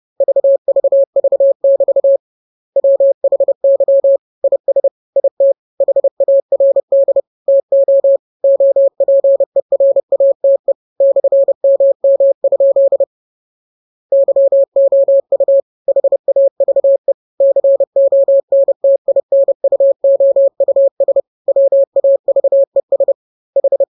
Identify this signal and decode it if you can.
Morse code, heard as a keyed tone, about 30 wpm, 550 Hz. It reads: VVV<BT> WHY IS IT HARD TO OPERATE /MM? YOU HAVE CONTINUOUS WAVES H
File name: jokes / joke 9 30 0000